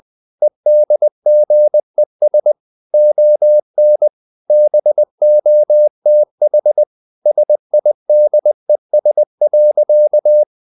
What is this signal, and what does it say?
Morse code, heard as a keyed tone, about 20 wpm, 600 Hz. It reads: EDGES ON BOTH SIDES.
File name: the old man and the sea part 2 20wpm